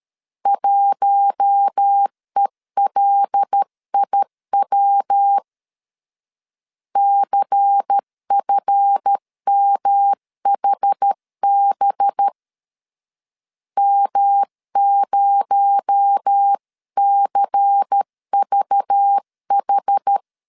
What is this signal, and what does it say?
Morse code, heard as a keyed tone, about 13 wpm, 800 Hz. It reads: 1ELIW CFMHB M0CVH
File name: LEZIONI / 7 GF9Z / 7 GF 12wpm 2122